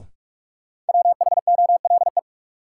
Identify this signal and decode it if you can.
Morse code, heard as a keyed tone, about 46 wpm, 700 Hz. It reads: WHOLE
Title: Sets of 2 Words - New Word Drill - 45wpm